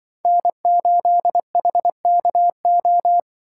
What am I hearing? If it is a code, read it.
Morse code, heard as a keyed tone, about 24 wpm, 700 Hz. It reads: N8HKO
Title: Speed Racing - USA 1x3 Call Signs 20wpm